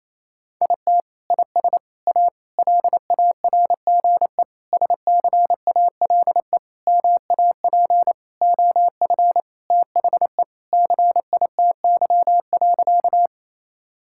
Morse code, heard as a keyed tone, about 28 wpm, 700 Hz. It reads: IT IS A LARGE SCALE MAP OF THE CITY.
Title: Sentences from Top 1000 Words - Review Part 5 - 28wpm